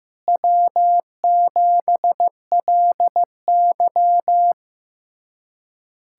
Morse code, heard as a keyed tone, about 15 wpm, 700 Hz. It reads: W7LY
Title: USA Callsign DE Callsign - 4 Characters - 15wpm